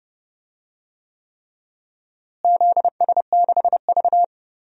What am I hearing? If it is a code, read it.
Morse code, heard as a keyed tone, about 32 wpm, 700 Hz. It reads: ZS6V
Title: DXCC-100 Call Signs - 4 Characters - Repeated 3x - 30wpm